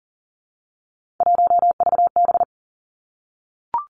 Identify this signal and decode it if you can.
Morse code, heard as a keyed tone, about 37 wpm, 700 Hz. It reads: 1VB
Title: Three Challenging Letter-Number Combinations 40wpm